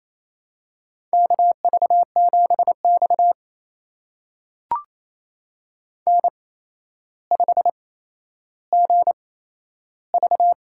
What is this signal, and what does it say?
Morse code, heard as a keyed tone, about 27 wpm, spaced as for 12 wpm, 700 Hz. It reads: KV7X N5GV
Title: USA Call Signs - 4 Characters - 8x Character Spacing - 28wpm